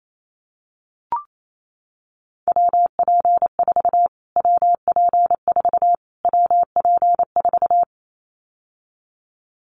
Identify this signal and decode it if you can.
Morse code, heard as a keyed tone, about 28 wpm, 700 Hz. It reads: WP4 WP4 WP4
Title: Callsign Prefixes for Top 25 DXCC - Repeated 3x - 28wpm